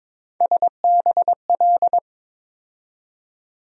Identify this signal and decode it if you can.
Morse code, heard as a keyed tone, about 22 wpm, 700 Hz. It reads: SBL
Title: USA Call Sign Postfixes - 3 Letters 22wpm